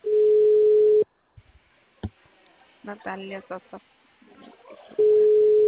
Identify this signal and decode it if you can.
Ringback tone (Europe, 425 Hz), with speech.